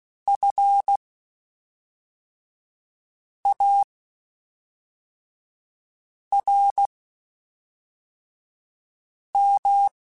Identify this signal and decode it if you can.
Morse code, heard as a keyed tone, about 16 wpm, spaced as for 3 wpm, 800 Hz. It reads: FARM
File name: General / General-mp3s / Houses @5